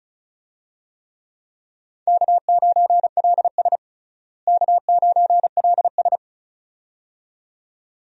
Morse code, heard as a keyed tone, about 36 wpm, 700 Hz. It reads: K9LS K9LS
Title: USA Call Signs - 4 Characters - Repeated 2x - 35wpm